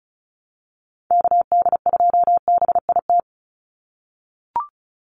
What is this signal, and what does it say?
Morse code, heard as a keyed tone, about 34 wpm, 700 Hz. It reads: KD2BIT
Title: USA Call Signs 35wpm